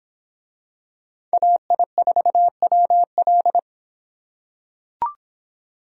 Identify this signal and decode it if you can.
Morse code, heard as a keyed tone, about 26 wpm, 700 Hz. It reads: AI4WL